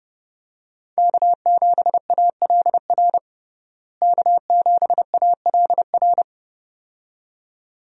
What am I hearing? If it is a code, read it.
Morse code, heard as a keyed tone, about 30 wpm, 700 Hz. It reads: K7ALR K7ALR
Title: USA Call Signs - 5 Characters - Repeated 2x - 30wpm